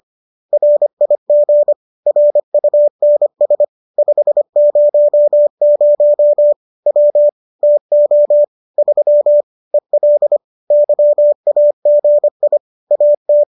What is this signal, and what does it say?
Morse code, heard as a keyed tone, about 25 wpm, 590 Hz. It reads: RIG RUNS 500 W TO 3 EL YAGI AT
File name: qso219 25